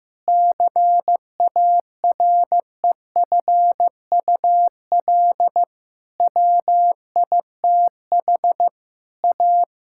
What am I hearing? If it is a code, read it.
Morse code, heard as a keyed tone, about 16 wpm, 700 Hz. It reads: CAREFUL WITH A
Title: Sentences from Top 1600 Words 15wpm